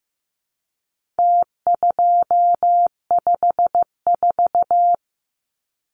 Morse code, heard as a keyed tone, about 15 wpm, 700 Hz. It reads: T254
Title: Numbers - Contest Style - 4-Digit 15wpm